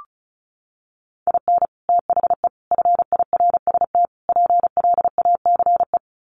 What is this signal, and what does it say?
Morse code, heard as a keyed tone, about 34 wpm, 700 Hz. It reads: IN THE FIRST PLACE
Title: Sets of 4 Words 35wpm